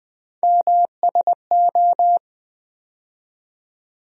Morse code, heard as a keyed tone, about 20 wpm, 700 Hz. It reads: MSO
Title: USA Call Sign Postfixes - 3 Letters 20wpm